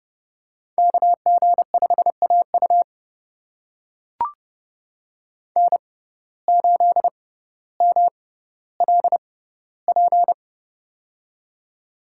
Morse code, heard as a keyed tone, about 30 wpm, spaced as for 15 wpm, 700 Hz. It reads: KG5AU N8MLP
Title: USA Call Signs - 5 Characters - 6x Character Spacing - 30wpm